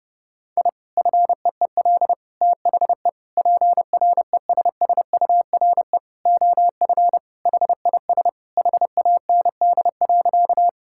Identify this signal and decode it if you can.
Morse code, heard as a keyed tone, about 30 wpm, 700 Hz. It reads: I FEEL THE PRESSURE OF HIS HAND.
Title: Sentences from Top 1000 Words - Review Part 6 - 30wpm